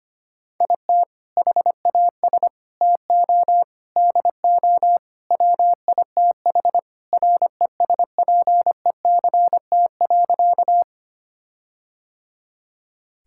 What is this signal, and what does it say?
Morse code, heard as a keyed tone, about 25 wpm, 700 Hz. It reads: IT HAS TO DO WITH RESPECT.
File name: practice / 25wpm / Sentences with 4-Word Phrases - 25wpm